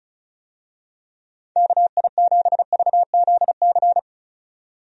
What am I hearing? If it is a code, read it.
Morse code, heard as a keyed tone, about 36 wpm, 700 Hz. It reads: KI7VZC
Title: USA Call Signs - 6 Characters - 35wpm